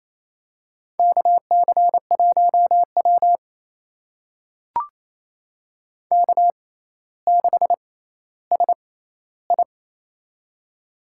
Morse code, heard as a keyed tone, about 28 wpm, spaced as for 14 wpm, 700 Hz. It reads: KC1W K6SI
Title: USA Call Signs - 4 Characters - 6x Character Spacing - 28wpm